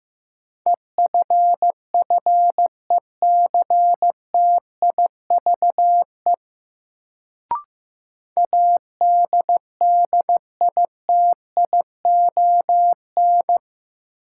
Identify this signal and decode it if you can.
Morse code, heard as a keyed tone, about 15 wpm, 700 Hz. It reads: EFFECTIVE ADDITION
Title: Sentences from Top 1000 Words - Review Part 6 - New Word Drill - 15wpm